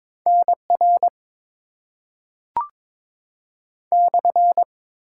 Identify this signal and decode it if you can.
Morse code, heard as a keyed tone, about 22 wpm, 700 Hz. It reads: N R /
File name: Top 90 QSO Elements - 6x Character Spacing - 22wpm